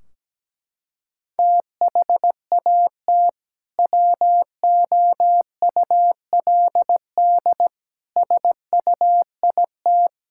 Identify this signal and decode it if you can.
Morse code, heard as a keyed tone, about 17 wpm, 700 Hz. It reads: THAT WOULD SUIT
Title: Sentences from Top 1600 Words 17wpm